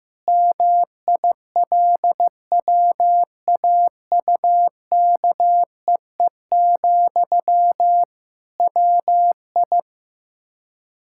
Morse code, heard as a keyed tone, about 15 wpm, 700 Hz. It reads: MILWAUKEE, WI
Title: Practice QSO - Exchange 1 - Beginning - 15wpm